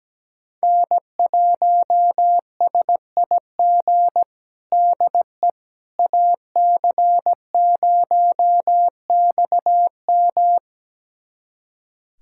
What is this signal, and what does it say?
Morse code, heard as a keyed tone, about 17 wpm, 700 Hz. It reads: N1SIG DE AC0XM
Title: USA Callsign DE Callsign - 5 Characters - 17wpm